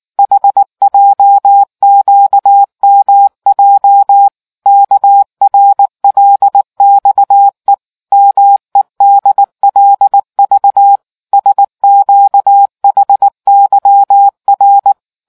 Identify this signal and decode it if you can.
Morse code, heard as a keyed tone, about 19 wpm, 800 Hz. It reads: HJQMJ KRLXE MEDLV SQHYR